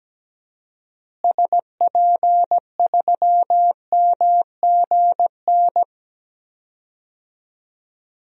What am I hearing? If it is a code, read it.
Morse code, heard as a keyed tone, about 17 wpm, 700 Hz. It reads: SP3MGN
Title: DXCC-100 Call Signs - 6 Characters - Repeated 3x - 17wpm